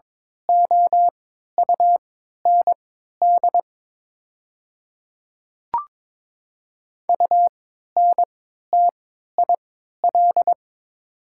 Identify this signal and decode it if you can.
Morse code, heard as a keyed tone, about 22 wpm, spaced as for 12 wpm, 700 Hz. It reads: OUND UNTIL
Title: 3x Character Spacing - Top 300 Words - 22wpm